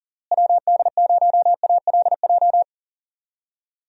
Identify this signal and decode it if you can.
Morse code, heard as a keyed tone, about 40 wpm, 700 Hz. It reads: WD0ALJ
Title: USA Call Signs - 6 Characters - Repeated 3x - 40wpm